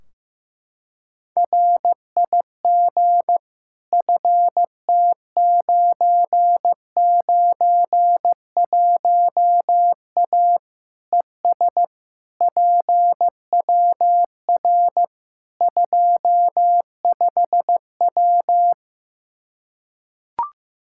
Morse code, heard as a keyed tone, about 15 wpm, 700 Hz. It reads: RIG FT991A ES PWR 25W
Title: Practice QSO - Exchange 2 - Info - 15wpm